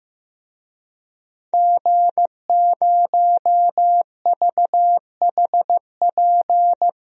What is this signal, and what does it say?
Morse code, heard as a keyed tone, about 15 wpm, 700 Hz. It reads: G0VHP
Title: DXCC-100 Call Signs - 5 Characters - 15wpm